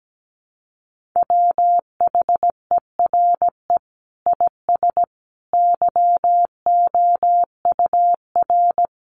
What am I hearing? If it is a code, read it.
Morse code, heard as a keyed tone, about 17 wpm, 700 Hz. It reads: WHERE IS YOUR
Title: Sentences from Top 1500 Words 17wpm